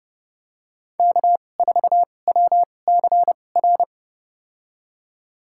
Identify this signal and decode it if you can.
Morse code, heard as a keyed tone, about 30 wpm, spaced as for 22 wpm, 700 Hz. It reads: K4WCR
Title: USA Call Signs - 5 Characters - 2x Character Spacing - 30wpm